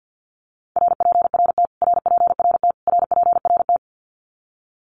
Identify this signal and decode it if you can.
Morse code, heard as a keyed tone, about 50 wpm, 700 Hz. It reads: RPRT RPRT RPRT
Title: Abbreviations Repeated 3x - 50wpm